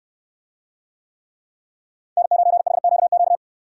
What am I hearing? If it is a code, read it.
Morse code, heard as a keyed tone, about 63 wpm, 700 Hz. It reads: N9HQQ